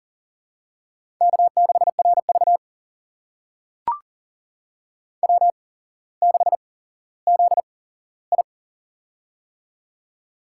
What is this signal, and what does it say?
Morse code, heard as a keyed tone, about 40 wpm, spaced as for 17 wpm, 700 Hz. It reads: K6RV W6ZI
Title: USA Call Signs - 4 Characters - 8x Character Spacing - 40wpm